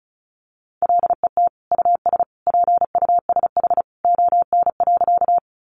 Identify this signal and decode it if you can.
Morse code, heard as a keyed tone, about 34 wpm, 700 Hz. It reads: LET US PUSH ON.
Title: Sentences from Top 1300 Words 35wpm